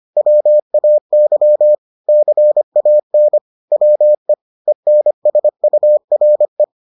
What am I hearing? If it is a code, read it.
Morse code, heard as a keyed tone, about 25 wpm, 600 Hz. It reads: WAY CAN WE ENSURE